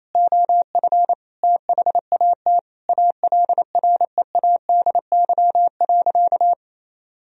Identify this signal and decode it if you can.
Morse code, heard as a keyed tone, about 28 wpm, 700 Hz. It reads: OF THAT ALREADY.
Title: Sentences from Top 1000 Words - Review Part 3 - 28wpm